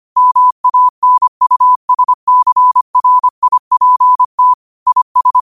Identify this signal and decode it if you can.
Morse code, heard as a keyed tone, about 25 wpm, 1000 Hz. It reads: MANUSCRIPT IS
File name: 25wpm